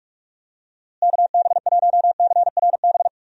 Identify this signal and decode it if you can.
Morse code, heard as a keyed tone, about 46 wpm, 700 Hz. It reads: KB1CRB
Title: USA Call Signs - 6 Characters - Repeated 2x - 45wpm